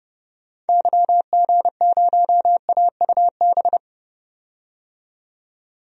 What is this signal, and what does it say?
Morse code, heard as a keyed tone, about 30 wpm, 690 Hz. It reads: YG0AUB
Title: DXCC-100 Call Signs - 6 Characters - 30wpm